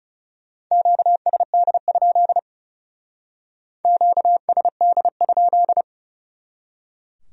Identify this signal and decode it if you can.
Morse code, heard as a keyed tone, about 39 wpm, 700 Hz. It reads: QSD? QSD?